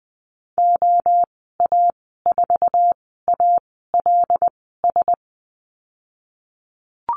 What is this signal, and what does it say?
Morse code, heard as a keyed tone, about 20 wpm, spaced as for 14 wpm, 700 Hz. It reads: OA4ALS
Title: 088 Lesson 15g - Number 5 - Call Signs 20wpm